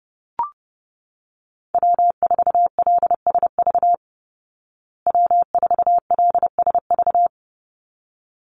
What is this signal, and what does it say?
Morse code, heard as a keyed tone, about 29 wpm, 700 Hz. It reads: W4LSV W4LSV
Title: Call Signs Repeated 3x - 30wpm